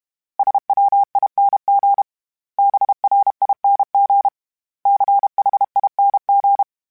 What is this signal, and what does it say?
Morse code, heard as a keyed tone, about 32 wpm, 800 Hz. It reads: SWING BRING CHING
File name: Special INGs-mp3s / Special INGs-mp3s / Sngl Ing @30